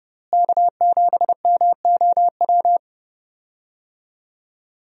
Morse code, heard as a keyed tone, about 30 wpm, 700 Hz. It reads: K7MOW
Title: USA Call Signs - 5 Characters - Repeated 2x - 30wpm